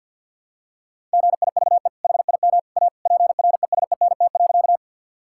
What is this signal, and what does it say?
Morse code, heard as a keyed tone, about 50 wpm, 700 Hz. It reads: GIVE HIM A PRESENT.
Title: Sentences from Top 1000 Words - Review Part 5 - 50wpm